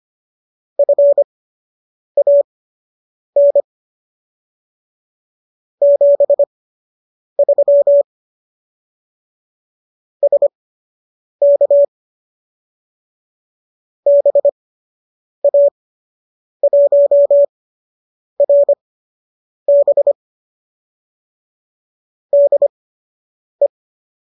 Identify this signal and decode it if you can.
Morse code, heard as a keyed tone, about 25 wpm, spaced as for 8 wpm, 570 Hz. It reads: FAN 73 SK BA1RB DE